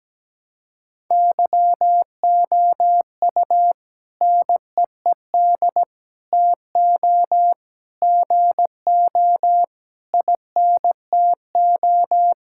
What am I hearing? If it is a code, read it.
Morse code, heard as a keyed tone, about 17 wpm, 700 Hz. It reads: YOU NEED TO GO INTO
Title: Sentences from Top 1000 Words - Review Part 6 - 17wpm